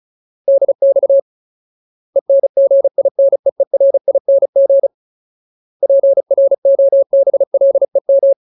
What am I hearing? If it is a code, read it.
Morse code, heard as a keyed tone, about 34 wpm, 550 Hz. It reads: DX ENGINEERING PROBLEM
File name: Ultimate Word List / 35WPM Ultimate